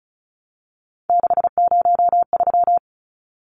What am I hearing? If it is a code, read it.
Morse code, heard as a keyed tone, about 34 wpm, 700 Hz. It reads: 603
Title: Numbers - 3-Digit 35wpm